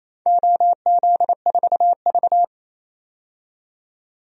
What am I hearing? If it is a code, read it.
Morse code, heard as a keyed tone, about 28 wpm, 700 Hz. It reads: OZ4V
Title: DXCC-100 Call Signs - 4 Characters - 28wpm